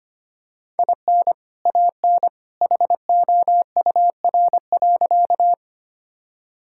Morse code, heard as a keyed tone, about 25 wpm, 700 Hz. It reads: IN AN HOUR.
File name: practice / 25wpm / Sentences with 2-Word Phrases - 25wpm